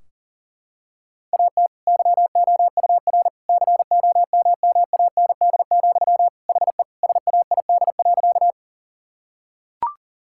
Morse code, heard as a keyed tone, about 40 wpm, 700 Hz. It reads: AT YOUR COMMAND, HE SAID.
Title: Sentences from Top 1600 Words 40wpm